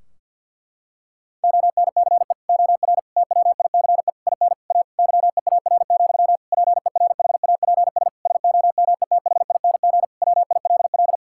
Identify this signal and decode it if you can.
Morse code, heard as a keyed tone, about 46 wpm, 700 Hz. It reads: ONCE OR TWICE IN A YEAR, PERHAPS SOMETHING WILL
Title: Sentences from Top 1600 Words 50wpm